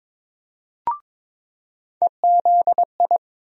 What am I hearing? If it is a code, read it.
Morse code, heard as a keyed tone, about 22 wpm, 700 Hz. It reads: EZI
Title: USA Call Sign Postfixes - 3 Letters 22wpm